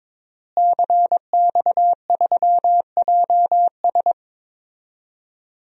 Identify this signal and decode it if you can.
Morse code, heard as a keyed tone, about 22 wpm, 700 Hz. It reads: CX3JS
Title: DXCC-100 Call Signs - 5 Characters - Repeated 2x - 22wpm